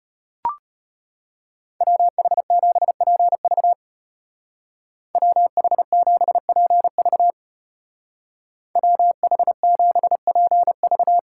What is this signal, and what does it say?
Morse code, heard as a keyed tone, about 32 wpm, 700 Hz. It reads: WH7PV WH7PV WH7PV